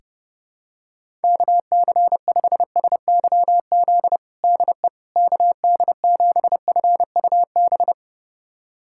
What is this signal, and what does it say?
Morse code, heard as a keyed tone, about 30 wpm, 700 Hz. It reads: KC5SYZ DE KD7FUB
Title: USA Callsign DE Callsign - 6 Characters - 30wpm